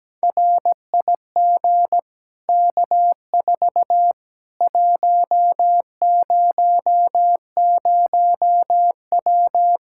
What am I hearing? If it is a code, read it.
Morse code, heard as a keyed tone, about 17 wpm, 700 Hz. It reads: RIG K4 100W
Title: Practice QSO - Exchange 2 - Info - 17wpm